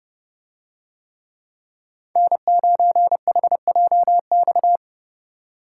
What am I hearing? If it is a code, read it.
Morse code, heard as a keyed tone, about 30 wpm, 700 Hz. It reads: N9HJX